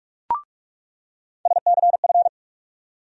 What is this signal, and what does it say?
Morse code, heard as a keyed tone, about 46 wpm, 700 Hz. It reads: SCF
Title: USA Call Sign Postfixes - 3 Letters 45wpm